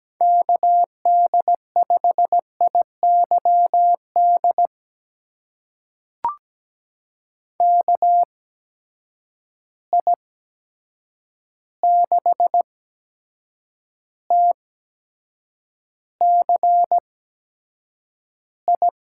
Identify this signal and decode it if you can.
Morse code, heard as a keyed tone, about 17 wpm, spaced as for 7 wpm, 700 Hz. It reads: KD5IYD KI6TCI
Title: USA Call Signs - 6 Characters - 8x Character Spacing - 17wpm